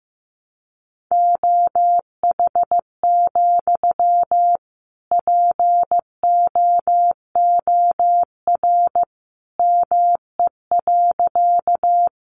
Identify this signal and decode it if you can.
Morse code, heard as a keyed tone, about 15 wpm, 700 Hz. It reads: OH, POOR ME.
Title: Sentences from Top 1000 Words 15wpm